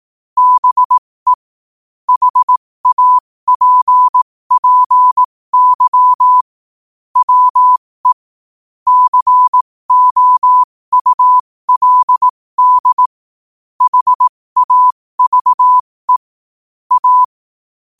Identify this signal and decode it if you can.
Morse code, heard as a keyed tone, about 18 wpm, spaced as for 15 wpm, 1000 Hz. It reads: BE HAPPY WE COULD HAVE A
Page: 15 WPM morse code quotes